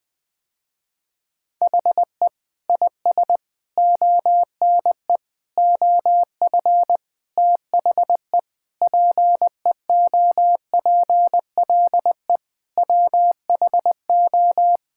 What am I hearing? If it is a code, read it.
Morse code, heard as a keyed tone, about 20 wpm, 700 Hz. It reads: HE IS ONE OF THE PEOPLE WHO
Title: Sentences with 4-Word Phrases - Spoken Beginning - 20wpm